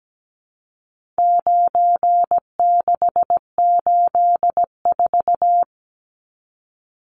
Morse code, heard as a keyed tone, about 17 wpm, 700 Hz. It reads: 9684
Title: Numbers - 4-Digit 17wpm